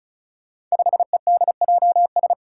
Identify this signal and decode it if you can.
Morse code, heard as a keyed tone, about 36 wpm, 700 Hz. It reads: 5EDJS
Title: Five Challenging Letter-Number Combinations 35wpm